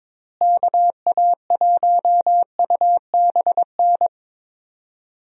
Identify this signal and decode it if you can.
Morse code, heard as a keyed tone, about 22 wpm, 700 Hz. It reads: KA1UBN
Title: USA Call Signs - 6 Characters - Repeated 2x - 22wpm